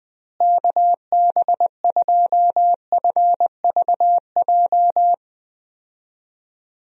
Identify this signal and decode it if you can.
Morse code, heard as a keyed tone, about 21 wpm, 700 Hz. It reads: KB2FVJ
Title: USA Call Signs - 6 Characters - Repeated 2x - 20wpm